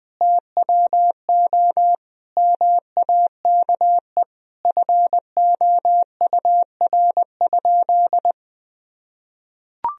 Morse code, heard as a keyed tone, about 20 wpm, 700 Hz. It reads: TWO MAKE FOUR?
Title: Sentences from Top 1000 Words - Review Part 1 - 20wpm